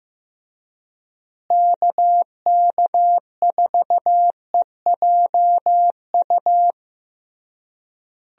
Morse code, heard as a keyed tone, about 15 wpm, 700 Hz. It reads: KK4EJU
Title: USA Call Signs - 6 Characters - Repeated 3x - 15wpm